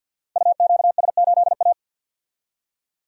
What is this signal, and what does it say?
Morse code, heard as a keyed tone, about 47 wpm, 700 Hz. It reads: UQS8A
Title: Code Groups - 5 Letter:Numbers - 50wpm